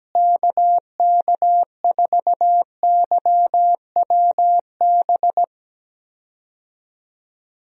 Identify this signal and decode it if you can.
Morse code, heard as a keyed tone, about 17 wpm, 700 Hz. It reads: KK4YWB
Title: USA Call Signs - 6 Characters - Repeated 3x - 17wpm